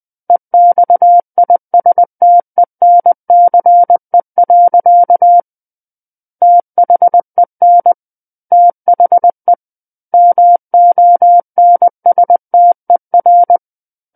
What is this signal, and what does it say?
Morse code, heard as a keyed tone, about 20 wpm, 700 Hz. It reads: EXISTENCE. THEN THE MONSTER
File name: 20 WPM